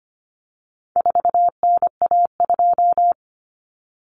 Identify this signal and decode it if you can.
Morse code, heard as a keyed tone, about 25 wpm, 700 Hz. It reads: 4NA2
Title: Common Cut Numbers - 4-Digit 25wpm